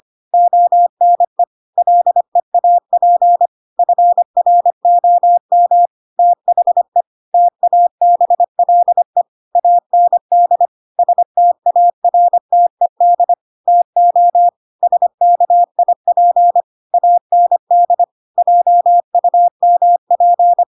Morse code, heard as a keyed tone, about 25 wpm, 700 Hz. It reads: ONE LEAP FROM THE TABLE AND STARTED TO SKIP AND JUMP
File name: Pinocchio / Pinocchio ch8 25 25 ik1hge